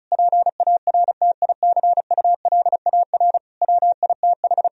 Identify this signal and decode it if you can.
Morse code, heard as a keyed tone, about 34 wpm, 700 Hz. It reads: PARTICULAR WITH
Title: Sentences from Top 1000 Words - Review Part 4 - 35wpm